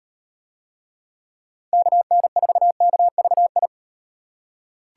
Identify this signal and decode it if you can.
Morse code, heard as a keyed tone, about 38 wpm, 700 Hz. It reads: KN4KVI